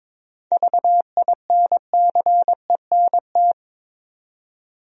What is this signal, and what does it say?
Morse code, heard as a keyed tone, about 22 wpm, 700 Hz. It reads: VINCENT
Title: USA Common Names - 22wpm